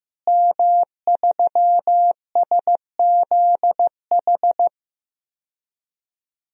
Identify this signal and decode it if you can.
Morse code, heard as a keyed tone, about 15 wpm, 700 Hz. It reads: M3SZH
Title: DXCC-100 Call Signs - 5 Characters - Repeated 3x - 15wpm